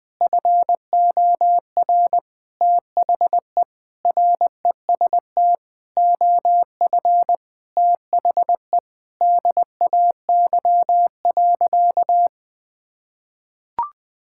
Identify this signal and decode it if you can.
Morse code, heard as a keyed tone, about 21 wpm, 700 Hz. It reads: FOR THE REST OF THE DAY.
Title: Sentences with 5-Word Phrases - 20wpm